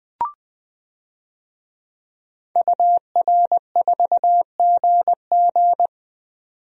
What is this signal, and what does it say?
Morse code, heard as a keyed tone, about 21 wpm, 700 Hz. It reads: UR4GG
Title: DXCC-100 Call Signs - 5 Characters - 20wpm